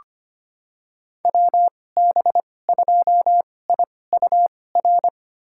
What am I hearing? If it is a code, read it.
Morse code, heard as a keyed tone, about 25 wpm, spaced as for 18 wpm, 700 Hz. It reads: WB2IUR
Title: USA Call Signs - 6 Characters - 2x Character Spacing - 25wpm